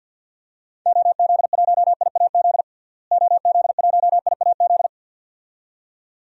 Morse code, heard as a keyed tone, about 48 wpm, 700 Hz. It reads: OZ1IAZ OZ1IAZ
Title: DXCC-100 Call Signs - 6 Characters - Repeated 2x - 50wpm